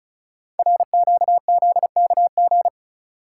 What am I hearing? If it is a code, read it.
Morse code, heard as a keyed tone, about 36 wpm, 700 Hz. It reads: RQZKG
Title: Code Groups - 5 Letter:Numbers - 35wpm